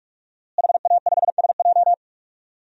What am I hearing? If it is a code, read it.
Morse code, heard as a keyed tone, about 41 wpm, 700 Hz. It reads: HA5SJ
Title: DXCC-100 Call Signs - 5 Characters - Repeated 3x - 45wpm